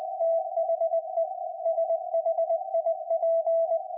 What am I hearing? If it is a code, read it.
Morse code, heard as a keyed tone, about 20 wpm, 660 Hz. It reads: THE SHIP
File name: IC-7100